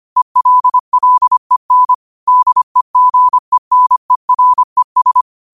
Morse code, heard as a keyed tone, about 25 wpm, 1000 Hz. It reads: ELLEN DEGENERES
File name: qotdfiles / 25wpm